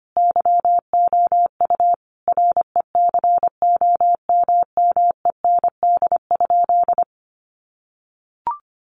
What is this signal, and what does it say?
Morse code, heard as a keyed tone, about 25 wpm, 700 Hz. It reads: YOU RECOMMEND?
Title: Sentences from Top 1200 Words 25wpm